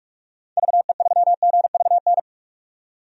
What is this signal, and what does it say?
Morse code, heard as a keyed tone, about 46 wpm, 700 Hz. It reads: VE3GVN
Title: DXCC-100 Call Signs - 6 Characters - 45wpm